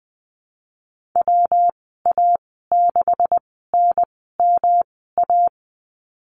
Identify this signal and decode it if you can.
Morse code, heard as a keyed tone, about 20 wpm, spaced as for 15 wpm, 700 Hz. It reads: WA6NMA
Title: 154 Lesson 25g - Letter p - Call Signs 20wpm